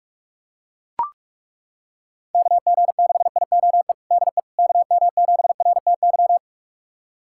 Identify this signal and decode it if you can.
Morse code, heard as a keyed tone, about 43 wpm, 690 Hz. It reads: KG6IOE DE KM7RTY
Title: USA Callsign DE Callsign 45wpm